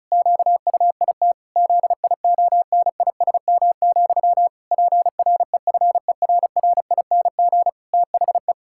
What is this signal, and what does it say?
Morse code, heard as a keyed tone, about 36 wpm, 700 Hz. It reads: QUIT ZIONISM, PREFERRING THE
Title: Pangram Challenge v2 - 35wpm